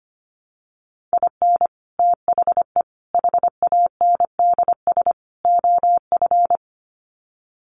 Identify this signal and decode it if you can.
Morse code, heard as a keyed tone, about 25 wpm, 700 Hz. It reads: IN THE HANDS OF
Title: Sets of 4 Words 25wpm